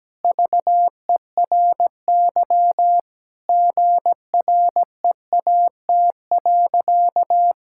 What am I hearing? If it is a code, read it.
Morse code, heard as a keyed tone, about 17 wpm, 700 Hz. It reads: VERY GREAT.
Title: Sentences from Top 1000 Words - Review Part 5 - 17wpm